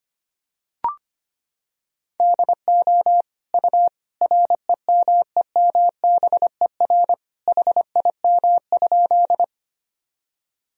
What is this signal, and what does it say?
Morse code, heard as a keyed tone, about 25 wpm, 700 Hz. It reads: DO U REMEMBER HIM?
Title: Sentences from Top 400 Words and Abbreviations 25wpm